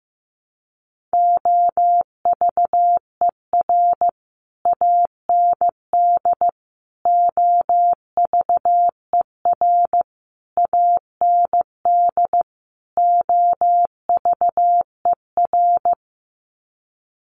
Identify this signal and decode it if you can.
Morse code, heard as a keyed tone, about 15 wpm, 700 Hz. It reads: OVER AND OVER AND OVER
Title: Sets of 5 Words Encore 15wpm